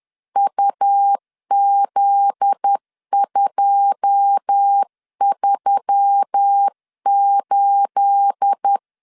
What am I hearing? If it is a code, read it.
Morse code, heard as a keyed tone, about 10 wpm, 800 Hz. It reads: UZ238
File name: LEZIONI / 9 28 10wpm 2842